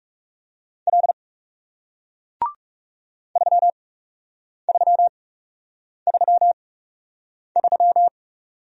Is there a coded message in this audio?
Yes: Morse code from 0.9 to 8.1 s, heard as a keyed tone, about 39 wpm, spaced as for 8 wpm, 700 Hz. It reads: L 3333